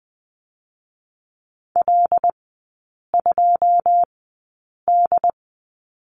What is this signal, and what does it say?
Morse code, heard as a keyed tone, about 20 wpm, spaced as for 8 wpm, 690 Hz. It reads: L2D
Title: 081 Lesson 14e - Number 2 - Three Characters 20wpm